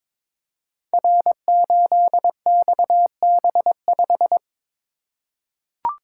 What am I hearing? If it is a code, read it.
Morse code, heard as a keyed tone, about 22 wpm, 700 Hz. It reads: R8XB5
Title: Code Groups - 5 Letter:Numbers - 22wpm